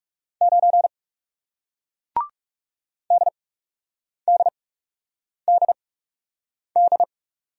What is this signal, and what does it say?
Morse code, heard as a keyed tone, about 39 wpm, spaced as for 8 wpm, 700 Hz. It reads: 9 DDDD